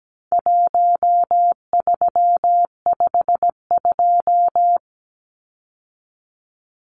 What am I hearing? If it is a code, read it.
Morse code, heard as a keyed tone, about 17 wpm, 700 Hz. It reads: 1352
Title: Numbers - Contest Style - 4-Digit 17wpm